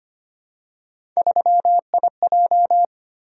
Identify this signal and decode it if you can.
Morse code, heard as a keyed tone, about 25 wpm, 700 Hz. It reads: 3IJ
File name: Three Letter-Number Combinations - 25wpm